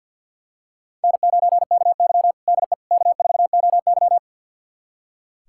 Morse code, heard as a keyed tone, about 46 wpm, 700 Hz. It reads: N9KY DE K4OY
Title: USA Callsign DE Callsign - 4 Characters - 50wpm